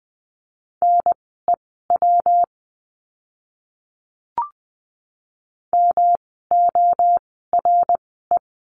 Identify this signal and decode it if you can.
Morse code, heard as a keyed tone, about 20 wpm, spaced as for 14 wpm, 700 Hz. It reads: NEW MORE
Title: 244 Lesson 39f - Letter z - Words 20wpm